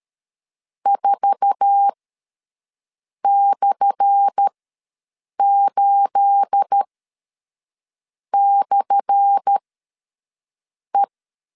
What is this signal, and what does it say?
Morse code, heard as a keyed tone, about 13 wpm, spaced as for 5 wpm, 800 Hz. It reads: 4/8/E
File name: LEZIONI / 10 SIGNS / 10A SLASH ws 12wpm 3443